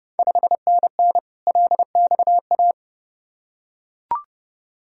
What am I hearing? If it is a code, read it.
Morse code, heard as a keyed tone, about 32 wpm, 700 Hz. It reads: 5NN LXA
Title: Ontario QSO Party Exchange 30wpm